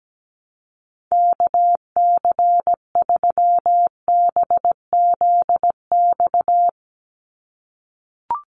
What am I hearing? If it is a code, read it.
Morse code, heard as a keyed tone, about 17 wpm, 700 Hz. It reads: KC3BZX
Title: USA Call Signs 17wpm